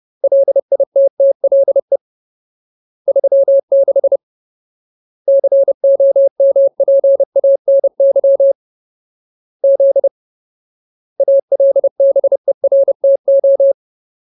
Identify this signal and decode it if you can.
Morse code, heard as a keyed tone, about 30 wpm, 550 Hz. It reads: LITTLE 36 COMPANY Z ALBERTO
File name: Ultimate Word List / 30WPM Ultimate